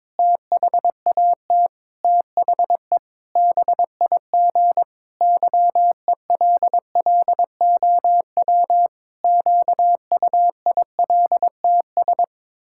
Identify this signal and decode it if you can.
Morse code, heard as a keyed tone, about 21 wpm, 700 Hz. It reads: THAT THE BIG YELLOW QUILTS
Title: Pangram Challenge v2 - 22wpm